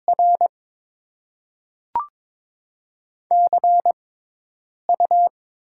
Morse code, heard as a keyed tone, about 22 wpm, 700 Hz. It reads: R C U
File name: Top 20 QSO Elements - 6x Character Spacing - 22wpm